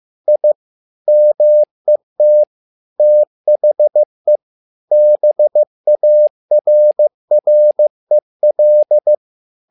Morse code, heard as a keyed tone, about 15 wpm, 600 Hz. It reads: I MET THE BARREL
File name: learning center / tower trouble / tower trouble 15wpm